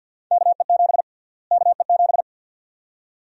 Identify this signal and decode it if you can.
Morse code, heard as a keyed tone, about 51 wpm, 700 Hz. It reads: KE7 KE7
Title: USA Call Sign Prefixes - Repeated 2x - 50wpm